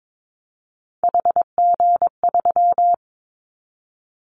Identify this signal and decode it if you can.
Morse code, heard as a keyed tone, about 22 wpm, 700 Hz. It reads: HG3
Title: Callsign Prefixes for Top 50 DXCC - 22wpm